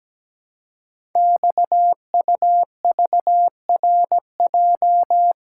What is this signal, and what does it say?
Morse code, heard as a keyed tone, about 17 wpm, 700 Hz. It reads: XUVRJ
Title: Code Groups - 5 Letters - 17wpm